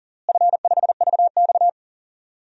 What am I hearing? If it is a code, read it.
Morse code, heard as a keyed tone, about 38 wpm, 700 Hz. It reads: F5VX
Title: DXCC-100 Call Signs - 4 Characters - Repeated 2x - 40wpm